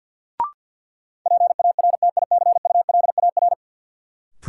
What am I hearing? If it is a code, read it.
Morse code, heard as a keyed tone, about 53 wpm, 700 Hz. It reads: PARTICULAR
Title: Sentences from Top 600 Words - New Word Drill - 50wpm